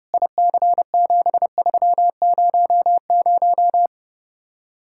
Morse code, heard as a keyed tone, about 37 wpm, 700 Hz. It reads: IC7300
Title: Speed-Racing Common QSO Phrases - 30wpm